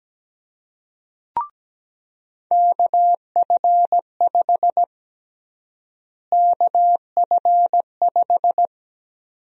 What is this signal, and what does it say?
Morse code, heard as a keyed tone, about 17 wpm, 700 Hz. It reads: KF5 KF5
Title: USA Call Sign Prefixes - Repeated 2x - 17wpm